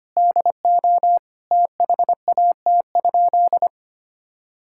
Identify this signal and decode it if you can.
Morse code, heard as a keyed tone, about 25 wpm, 700 Hz. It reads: DO THAT?
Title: Sentences from Top 1000 Words - Review Part 6 - 25wpm